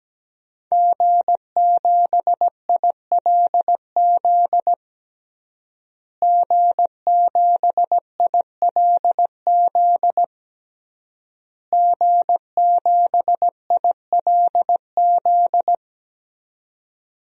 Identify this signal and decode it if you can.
Morse code, heard as a keyed tone, about 17 wpm, 700 Hz. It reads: G7ILZ G7ILZ G7ILZ
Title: DXCC-100 Call Signs - 5 Characters - Repeated 3x - 17wpm